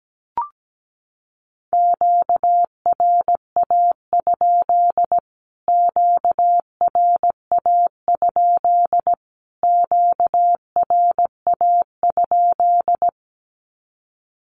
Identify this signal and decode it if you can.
Morse code, heard as a keyed tone, about 17 wpm, 700 Hz. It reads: QRA? QRA? QRA?
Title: Q-Codes Repeated 3x 17wpm